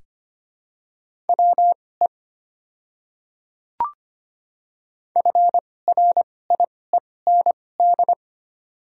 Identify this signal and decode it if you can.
Morse code, heard as a keyed tone, about 25 wpm, spaced as for 18 wpm, 700 Hz. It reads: WE FRIEND
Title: 2x Character Spacing - Top 500 Words - 25wpm